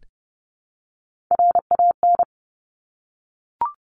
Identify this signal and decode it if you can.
Morse code, heard as a keyed tone, about 32 wpm, 690 Hz. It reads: RAN
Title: Top Three Letter Combinations in Words 30wpm